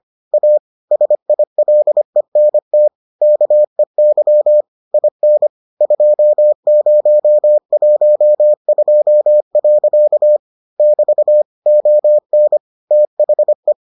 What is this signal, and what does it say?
Morse code, heard as a keyed tone, about 25 wpm, 600 Hz. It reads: A SILENT KEY IN 2012. <BT> ON THE